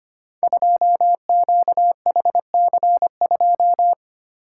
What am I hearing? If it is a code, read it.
Morse code, heard as a keyed tone, about 25 wpm, 700 Hz. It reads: 2QHC2
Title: Code Groups - 5 Letter:Numbers - 25wpm